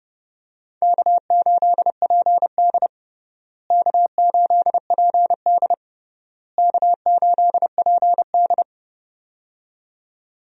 Morse code, heard as a keyed tone, about 31 wpm, 700 Hz. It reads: K8PD K8PD K8PD
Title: USA Call Signs - 4 Characters - Repeated 3x - 30wpm